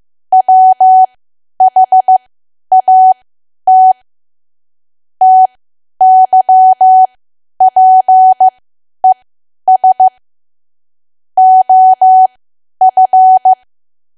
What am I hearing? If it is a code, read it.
Morse code, heard as a keyed tone, about 15 wpm, spaced as for 10 wpm, 750 Hz. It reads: WHAT TYPES OF
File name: cwa / W1AW-Level2 / 10WPM